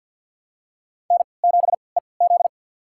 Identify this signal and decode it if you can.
Morse code, heard as a keyed tone, about 53 wpm, spaced as for 29 wpm, 700 Hz. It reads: N7EZ
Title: USA Call Signs - 4 Characters - 3x Character Spacing - 50wpm